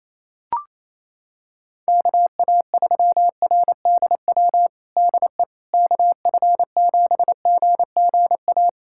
Morse code, heard as a keyed tone, about 28 wpm, 700 Hz. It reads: KA3RDW DE KF7GGA
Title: USA Callsign DE Callsign 28wpm